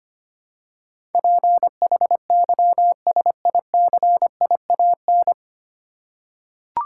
Morse code, heard as a keyed tone, about 25 wpm, 700 Hz. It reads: PHYSICIAN
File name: Nine Letter Words - 25wpm